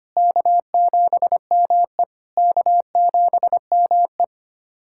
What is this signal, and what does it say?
Morse code, heard as a keyed tone, about 25 wpm, 700 Hz. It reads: K7ME K7ME
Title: SOTA QSO Practice 25wpm